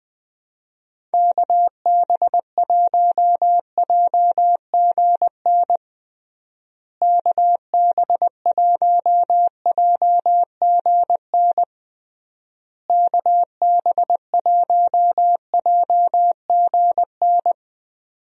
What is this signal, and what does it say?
Morse code, heard as a keyed tone, about 20 wpm, 700 Hz. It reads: KB1JGN KB1JGN KB1JGN
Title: USA Call Signs - 6 Characters - Repeated 3x - 20wpm